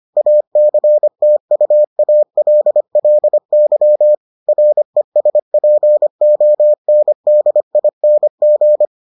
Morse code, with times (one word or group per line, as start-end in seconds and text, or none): 0.2-4.1 ACTUALLY
4.5-8.9 RESPONDING